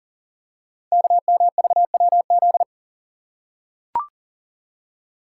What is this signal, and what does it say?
Morse code, heard as a keyed tone, about 43 wpm, 700 Hz. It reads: KMVWZ
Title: Code Groups - 5 Letters - 40wpm